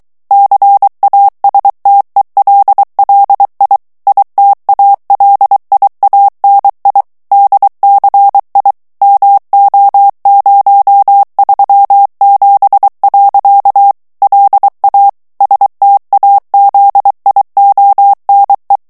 Morse code, heard as a keyed tone, about 23 wpm, 800 Hz. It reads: CASTELLI ITALIANI DCI MO037. LA STAZIONE
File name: morse mp3 / ITALIEN texte clair 1 vitesse 22 mots minute